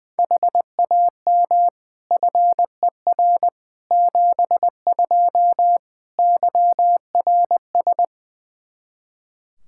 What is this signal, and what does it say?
Morse code, heard as a keyed tone, about 20 wpm, 700 Hz. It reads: HAM FER 72 YRS
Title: Practice QSO - Exchange 3 - More Info - 20wpm